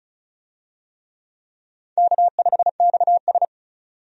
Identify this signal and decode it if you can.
Morse code, heard as a keyed tone, about 36 wpm, 700 Hz. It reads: K5XS
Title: USA Call Signs - 4 Characters - 35wpm